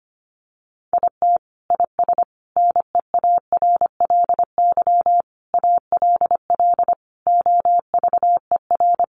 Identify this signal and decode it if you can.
Morse code, heard as a keyed tone, about 25 wpm, 700 Hz. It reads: IT IS NEARLY ALL OVER
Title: Sentences from Top 1300 Words 25wpm